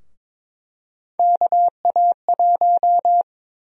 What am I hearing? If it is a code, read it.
Morse code, heard as a keyed tone, about 22 wpm, 700 Hz. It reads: KA1
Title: USA Call Sign Prefixes - Repeated 2x - 22wpm